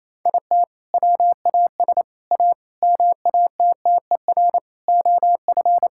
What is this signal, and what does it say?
Morse code, heard as a keyed tone, about 27 wpm, 700 Hz. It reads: IT WAS A MATTER OF
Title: Sentences from Top 1000 Words - Review Part 5 - 28wpm